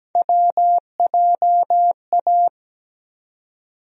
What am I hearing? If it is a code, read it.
Morse code, heard as a keyed tone, about 17 wpm, 700 Hz. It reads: WJA
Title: USA Call Sign Postfixes - 3 Letters 17wpm